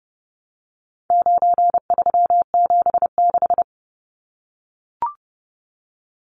Morse code, heard as a keyed tone, about 30 wpm, 700 Hz. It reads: 9376
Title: Numbers - 4-Digit 30wpm